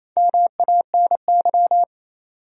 Morse code, heard as a keyed tone, about 28 wpm, 700 Hz. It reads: MANY
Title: Top 100 Words 28wpm